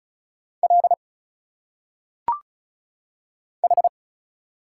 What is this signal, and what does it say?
Morse code, heard as a keyed tone, about 34 wpm, 700 Hz. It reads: L H
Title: Single Letters - Phonetic Pronunciation 35wpm